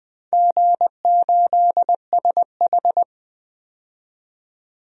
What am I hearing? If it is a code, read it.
Morse code, heard as a keyed tone, about 20 wpm, 700 Hz. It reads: G8SH
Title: DXCC-100 Call Signs - 4 Characters - 20wpm